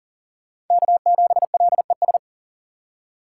Morse code, heard as a keyed tone, about 40 wpm, 700 Hz. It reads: K7LES
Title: USA Call Signs - 5 Characters - 6x Character Spacing - 40wpm